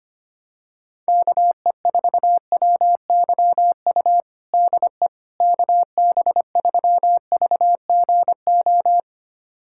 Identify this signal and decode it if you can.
Morse code, heard as a keyed tone, about 25 wpm, 700 Hz. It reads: KE4WYU DE KB3VGO
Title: USA Callsign DE Callsign - 6 Characters - 25wpm